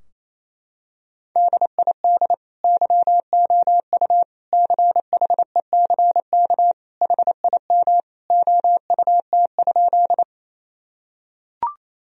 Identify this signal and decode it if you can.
Morse code, heard as a keyed tone, about 28 wpm, 700 Hz. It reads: DID YOU CHECK HIM OUT?
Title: Sentences from Top 1000 Words - Review Part 3 - 28wpm